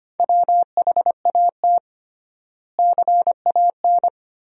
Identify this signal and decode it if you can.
Morse code, heard as a keyed tone, about 25 wpm, 700 Hz. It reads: WHAT CAN
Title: 3x Word-Spacing - Sentences from Top 400 Words - 25wpm